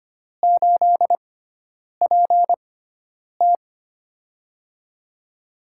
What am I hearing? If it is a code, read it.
Morse code, heard as a keyed tone, about 25 wpm, spaced as for 9 wpm, 700 Hz. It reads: 8PT
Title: USA Call Signs - 4 Characters - 6x Character Spacing - 25wpm